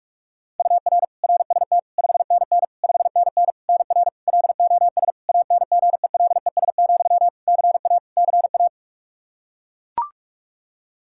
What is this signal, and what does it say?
Morse code, heard as a keyed tone, about 41 wpm, 700 Hz. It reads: UR RST 5NN 5NN NR LOS ANGELES, CA CA
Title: K1USN SST QSO Practice 45wpm